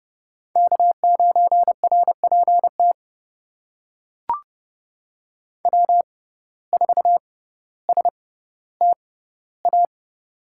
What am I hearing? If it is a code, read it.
Morse code, heard as a keyed tone, about 30 wpm, spaced as for 15 wpm, 700 Hz. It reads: K9RPT W4STA
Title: USA Call Signs - 5 Characters - 6x Character Spacing - 30wpm